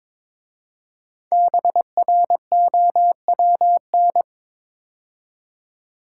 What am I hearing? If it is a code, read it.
Morse code, heard as a keyed tone, about 22 wpm, 700 Hz. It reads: BROWN
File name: 5 Letter Words - First 3 Letters Given - 22wpm